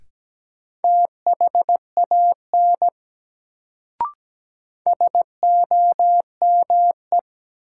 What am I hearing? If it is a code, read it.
Morse code, heard as a keyed tone, about 17 wpm, 700 Hz. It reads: THAN SOME
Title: Sentences from Top 100 Words - New Word Drill - 17wpm